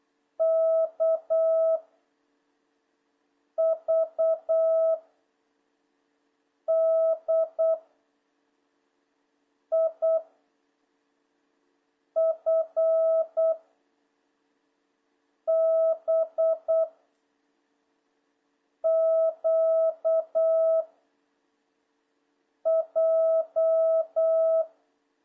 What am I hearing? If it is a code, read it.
Morse code, heard as a keyed tone, about 8 wpm, 660 Hz. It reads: K V D I F B Q J
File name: jh8cbh / 2026cw